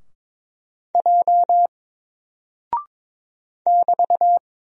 Morse code, heard as a keyed tone, about 22 wpm, 700 Hz. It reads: J <BT>
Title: Single Character (Letter:Number:Punctuation:Prosign) Alt 22wpm